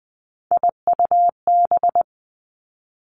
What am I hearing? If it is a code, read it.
Morse code, heard as a keyed tone, about 20 wpm, 700 Hz. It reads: IUB
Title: Three Challenging Letter-Number Combinations 20wpm